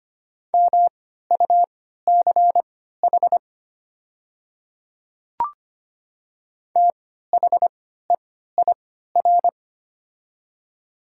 Morse code, heard as a keyed tone, about 25 wpm, spaced as for 14 wpm, 700 Hz. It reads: MUCH THEIR
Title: 3x Character Spacing - Top 200 Words - 25wpm